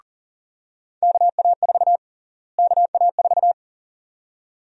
Morse code, heard as a keyed tone, about 40 wpm, 700 Hz. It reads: KA4 KA4
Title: USA Call Sign Prefixes - Repeated 2x - 40wpm